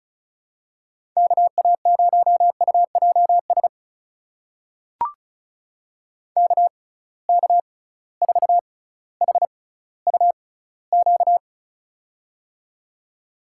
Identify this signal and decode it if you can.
Morse code, heard as a keyed tone, about 34 wpm, spaced as for 18 wpm, 700 Hz. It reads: KA0UJS KK4HUQ
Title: USA Call Signs - 6 Characters - 6x Character Spacing - 35wpm